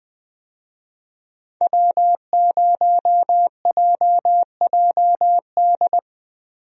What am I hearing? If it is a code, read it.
Morse code, heard as a keyed tone, about 21 wpm, 700 Hz. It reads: W0JJD
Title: USA Call Signs - 5 Characters - Repeated 3x - 20wpm